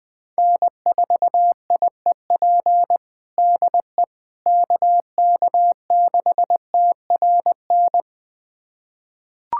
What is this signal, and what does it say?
Morse code, heard as a keyed tone, about 21 wpm, 700 Hz. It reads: N4IEP DE KK6TRN
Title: USA Callsign DE Callsign 20wpm